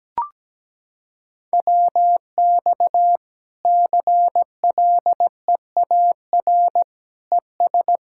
Morse code, heard as a keyed tone, about 17 wpm, 700 Hz. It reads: WX CLEAR ES
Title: K1USN SST QSO Practice 17wpm